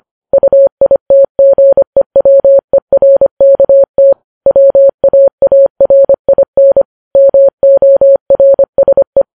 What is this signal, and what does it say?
Morse code, heard as a keyed tone, about 25 wpm, 550 Hz. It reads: UITGEWERKT WAARIN MORSE
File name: Files Web tls / Morse on9fv / morse25kps